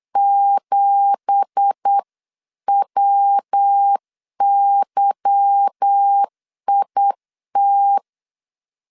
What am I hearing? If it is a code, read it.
Morse code, heard as a keyed tone, about 8 wpm, 800 Hz. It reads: 7WYIT
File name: CW / DET / 1 9 1500C S C 8wpm 5653